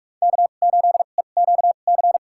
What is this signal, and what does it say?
Morse code, heard as a keyed tone, about 43 wpm, spaced as for 32 wpm, 700 Hz. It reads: K8EQC
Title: USA Call Signs - 5 Characters - 2x Character Spacing - 45wpm